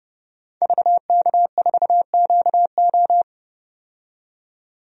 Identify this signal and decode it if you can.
Morse code, heard as a keyed tone, about 32 wpm, 700 Hz. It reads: VK4QO
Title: DXCC-100 Call Signs - 5 Characters - 30wpm